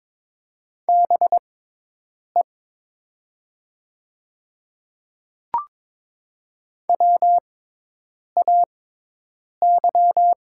Morse code, heard as a keyed tone, about 22 wpm, spaced as for 8 wpm, 700 Hz. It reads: BE WAY
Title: 6x Character Spacing - Top 200 Words - 22wpm